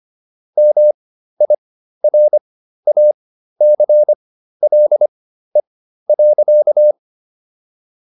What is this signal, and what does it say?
Morse code, heard as a keyed tone, about 25 wpm, spaced as for 13 wpm, 600 Hz. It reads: MIRACLE.